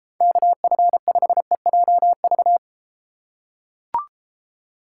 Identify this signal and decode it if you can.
Morse code, heard as a keyed tone, about 32 wpm, 700 Hz. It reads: KF5EJV